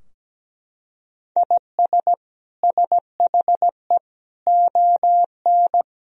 Morse code, heard as a keyed tone, about 17 wpm, 700 Hz. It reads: IS SHE ON
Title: Sentences from Top 1000 Words - Review Part 3 - 17wpm